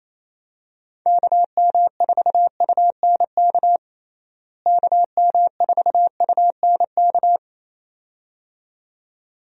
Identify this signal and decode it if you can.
Morse code, heard as a keyed tone, about 28 wpm, 700 Hz. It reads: KM4UNK KM4UNK
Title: USA Call Signs - 6 Characters - Repeated 2x - 28wpm